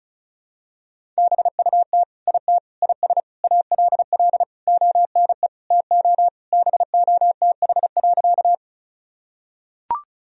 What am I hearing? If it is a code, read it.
Morse code, heard as a keyed tone, about 34 wpm, 690 Hz. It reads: BUT IT IS ALL ONE TO BOTH.
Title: Sentences from Top 1000 Words - Review Part 1 - 35wpm